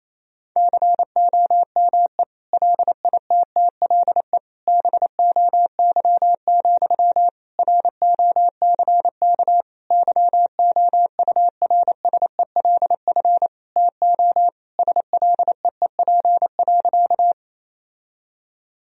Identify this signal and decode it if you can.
Morse code, heard as a keyed tone, about 28 wpm, 700 Hz. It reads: COME LITTLE BOY, ROCK YOURSELF TO SLEEP.
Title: Sentences from Top 1000 Words - Review Part 2 - 28wpm